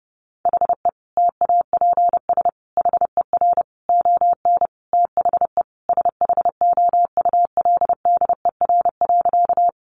Morse code, heard as a keyed tone, about 30 wpm, 700 Hz. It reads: HE TAPS HER ON THE SHOULDER.
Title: Additional Easy Phrases 30wpm